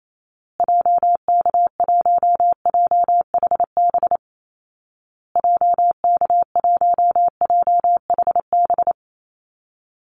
Morse code, heard as a keyed tone, about 28 wpm, 700 Hz. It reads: JK1JHB JK1JHB
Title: Call Signs Repeated 2x - 28wpm